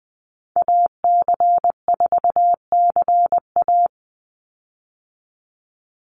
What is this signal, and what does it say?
Morse code, heard as a keyed tone, about 20 wpm, 700 Hz. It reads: AC4CA
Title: Call Signs Repeated 3x - 20wpm